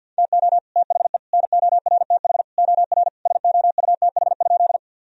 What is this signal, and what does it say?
Morse code, heard as a keyed tone, about 50 wpm, 700 Hz. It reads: TO THE NORTH OR SOUTH?
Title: Sentences from Top 1000 Words - Review Part 2 - 50wpm